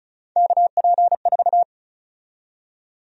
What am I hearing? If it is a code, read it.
Morse code, heard as a keyed tone, about 34 wpm, 700 Hz. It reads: KP4
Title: USA Call Sign Prefixes - 35wpm